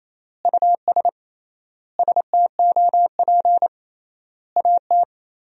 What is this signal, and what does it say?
Morse code, heard as a keyed tone, about 28 wpm, 700 Hz. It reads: US STOP AT
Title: 3x Word-Spacing - Sentences from Top 400 Words - 28wpm